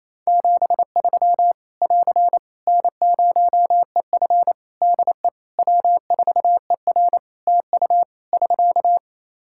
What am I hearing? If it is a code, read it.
Morse code, heard as a keyed tone, about 28 wpm, 700 Hz. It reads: 73 <AR> N0EF DE W4ER TU <SK>
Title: Practice QSO - Exchange 4 - Ending - 28wpm